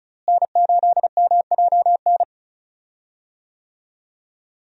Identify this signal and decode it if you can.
Morse code, heard as a keyed tone, about 36 wpm, 700 Hz. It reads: N8MJN
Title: USA Call Signs - 5 Characters - Repeated 3x - 35wpm